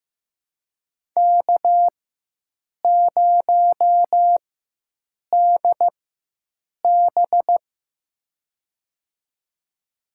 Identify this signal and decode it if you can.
Morse code, heard as a keyed tone, about 15 wpm, spaced as for 7 wpm, 700 Hz. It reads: K0DB
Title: USA Call Signs - 4 Characters - 4x Character Spacing - 15wpm